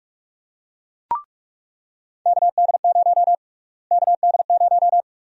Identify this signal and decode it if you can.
Morse code, heard as a keyed tone, about 43 wpm, 700 Hz. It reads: KD0 KD0
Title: USA Call Sign Prefixes - Repeated 2x - 45wpm